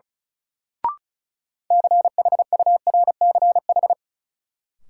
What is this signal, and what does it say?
Morse code, heard as a keyed tone, about 36 wpm, 700 Hz. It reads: CHURCH
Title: Sets of 5 Words - New Word Drill - 35wpm